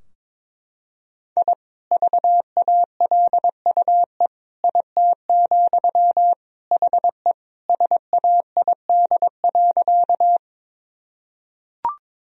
Morse code, heard as a keyed tone, about 22 wpm, 700 Hz. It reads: I VALUE IT, HE SAID.
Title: Sentences from Top 1000 Words - Review Part 3 - 22wpm